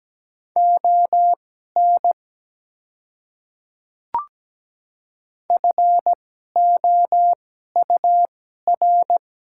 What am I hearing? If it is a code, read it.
Morse code, heard as a keyed tone, about 17 wpm, spaced as for 12 wpm, 700 Hz. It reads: ON FOUR
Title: 2x Character Spacing - Top 300 Words - 17wpm